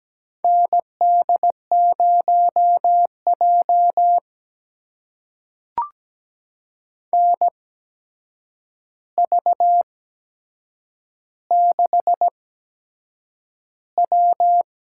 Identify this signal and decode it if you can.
Morse code, heard as a keyed tone, about 17 wpm, spaced as for 7 wpm, 700 Hz. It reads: ND0J NV6W
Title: USA Call Signs - 4 Characters - 8x Character Spacing - 17wpm